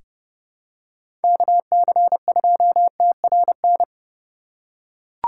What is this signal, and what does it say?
Morse code, heard as a keyed tone, about 30 wpm, 700 Hz. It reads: KC2TRN
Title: USA Call Signs - 6 Characters - 3x Character Spacing - 30wpm